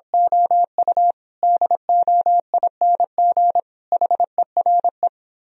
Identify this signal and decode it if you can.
Morse code, heard as a keyed tone, about 26 wpm, 700 Hz. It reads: OU DOING HERE
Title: Speed-Racing - Sets of 5 Words 22wpm